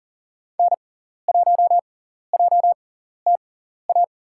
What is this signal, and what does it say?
Morse code, heard as a keyed tone, about 40 wpm, spaced as for 14 wpm, 700 Hz. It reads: N1JTA
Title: USA Call Signs - 5 Characters - 6x Character Spacing - 40wpm